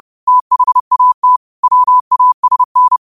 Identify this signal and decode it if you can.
Morse code, heard as a keyed tone, about 30 wpm, 1000 Hz. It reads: THAT WASN
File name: qotdfiles / 30wpm